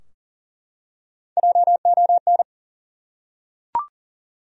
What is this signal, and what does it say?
Morse code, heard as a keyed tone, about 40 wpm, 700 Hz. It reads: JON
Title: USA Common Names - Max 3 Letters - 40wpm